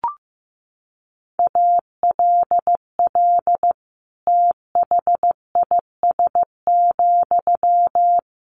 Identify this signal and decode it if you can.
Morse code, heard as a keyed tone, about 15 wpm, 700 Hz. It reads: ALL THIS,
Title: Sentences from Top 600 Words 15wpm